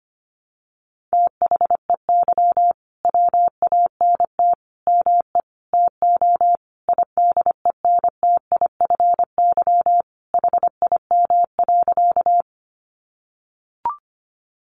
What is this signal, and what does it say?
Morse code, heard as a keyed tone, about 25 wpm, 700 Hz. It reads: THEY WANT ME TO IDENTIFY HIM.
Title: Sentences from Top 900 Words 25wpm